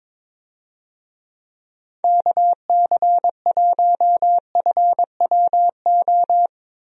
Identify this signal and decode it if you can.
Morse code, heard as a keyed tone, about 22 wpm, 700 Hz. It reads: KC1FWO
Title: USA Call Signs - 6 Characters - Repeated 3x - 22wpm